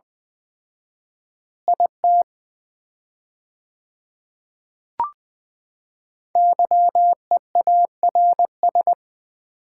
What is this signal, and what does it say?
Morse code, heard as a keyed tone, about 20 wpm, 700 Hz. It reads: IT YEARS
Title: 4x Word-Spacing - Sets of 2 Words - 20wpm